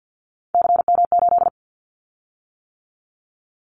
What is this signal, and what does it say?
Morse code, heard as a keyed tone, about 50 wpm, 690 Hz. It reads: CM8
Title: Callsign Prefixes for Top 100 DXCC - 50wpm